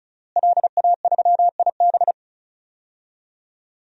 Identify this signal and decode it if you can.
Morse code, heard as a keyed tone, about 36 wpm, 700 Hz. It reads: LA3IB
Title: DXCC-100 Call Signs - 5 Characters - Repeated 3x - 35wpm